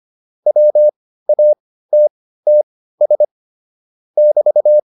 Morse code, heard as a keyed tone, about 25 wpm, spaced as for 15 wpm, 600 Hz. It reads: WATTS <BT>